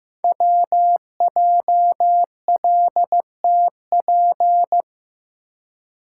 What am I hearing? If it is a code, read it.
Morse code, heard as a keyed tone, about 15 wpm, 700 Hz. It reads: WJLTP
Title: Code Groups - 5 Letters - 15wpm